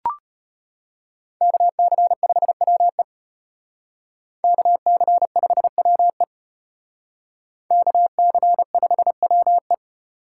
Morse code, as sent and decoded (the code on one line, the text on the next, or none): -.- -.-. ..... .-- . / -.- -.-. ..... .-- . / -.- -.-. ..... .-- .
KC5WE KC5WE KC5WE